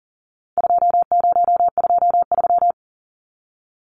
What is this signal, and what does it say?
Morse code, heard as a keyed tone, about 40 wpm, 700 Hz. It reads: 2023
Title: Numbers - 4-Digit 40wpm